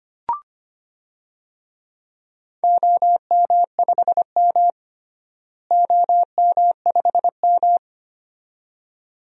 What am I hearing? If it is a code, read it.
Morse code, heard as a keyed tone, about 25 wpm, 700 Hz. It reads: OM5M OM5M
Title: DXCC-100 Call Signs - 4 Characters - Repeated 2x - 25wpm